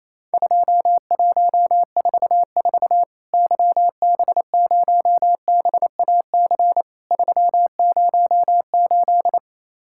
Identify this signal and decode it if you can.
Morse code, heard as a keyed tone, about 28 wpm, 700 Hz. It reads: 2144 YB0BAC 308
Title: Worked All Europe DX Contest - QTC Exchange - 28wpm